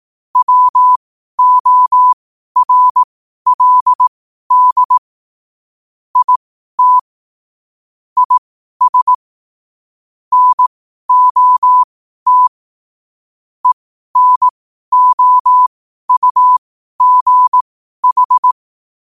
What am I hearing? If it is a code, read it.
Morse code, heard as a keyed tone, about 18 wpm, spaced as for 12 wpm, 1000 Hz. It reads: WORLD IT IS NOT ENOUGH
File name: qotdfiles / 12wpm